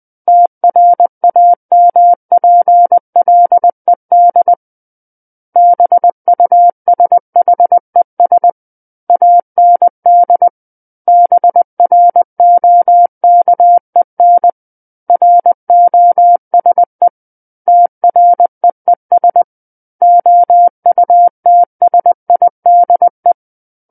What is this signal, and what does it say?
Morse code, heard as a keyed tone, about 20 wpm, 700 Hz. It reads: TRAMPLED BUSHES AND BROKEN ROSE TREES OUTSIDE
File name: 20 WPM